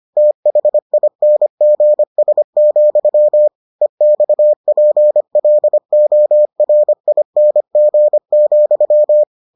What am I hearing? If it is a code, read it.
Morse code, heard as a keyed tone, about 25 wpm, 600 Hz. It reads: THINGS, EXPLORING,